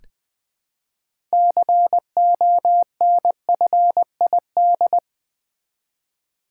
Morse code, heard as a keyed tone, about 20 wpm, 700 Hz. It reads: CONFID
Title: Build A Word - 3 Letters At A Time - 9-Letter Words - 20wpm